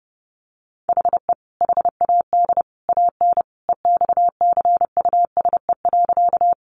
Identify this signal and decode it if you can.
Morse code, heard as a keyed tone, about 30 wpm, 700 Hz. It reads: HE HAD AN EXCUSE.
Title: Sentences from Top 800 Words 30wpm